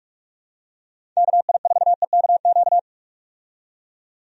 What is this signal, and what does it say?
Morse code, heard as a keyed tone, about 47 wpm, 700 Hz. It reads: KI4EKQ
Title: USA Call Signs - 6 Characters - 45wpm